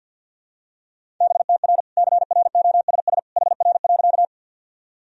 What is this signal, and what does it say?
Morse code, heard as a keyed tone, about 46 wpm, 700 Hz. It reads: BTR CROSS HR.
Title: Sentences from Top 400 Words and Abbreviations 50wpm